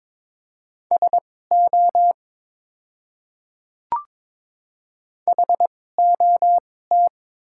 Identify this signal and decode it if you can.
Morse code, heard as a keyed tone, about 22 wpm, spaced as for 16 wpm, 700 Hz. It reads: SO HOT
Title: 2x Character Spacing - Top 100 Words - 22wpm